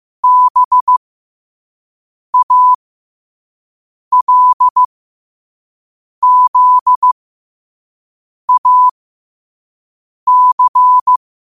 Morse code, heard as a keyed tone, about 15 wpm, spaced as for 5 wpm, 1000 Hz. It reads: BALZAC
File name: qotdfiles / 5wpm